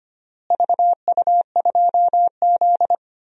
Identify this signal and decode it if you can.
Morse code, heard as a keyed tone, about 25 wpm, 700 Hz. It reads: VU2Z
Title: DXCC-100 Call Signs - 4 Characters - Repeated 3x - 25wpm